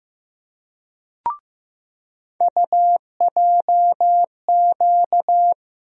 Morse code, heard as a keyed tone, about 15 wpm, 700 Hz. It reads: UJQ
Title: USA Call Sign Postfixes - 3 Letters 15wpm